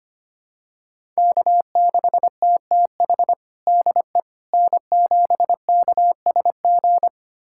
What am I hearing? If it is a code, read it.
Morse code, heard as a keyed tone, about 25 wpm, 700 Hz. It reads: K6TTH DE N7KSG
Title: USA Callsign DE Callsign - 5 Characters - 25wpm